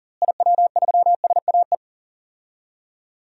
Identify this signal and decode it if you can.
Morse code, heard as a keyed tone, about 37 wpm, 700 Hz. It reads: IW3SAE
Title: DXCC-100 Call Signs - 6 Characters - Repeated 3x - 40wpm